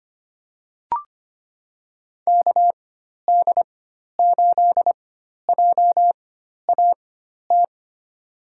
Morse code, heard as a keyed tone, about 25 wpm, spaced as for 12 wpm, 700 Hz. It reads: KD8JAT
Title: USA Call Signs - 6 Characters - 4x Character Spacing - 25wpm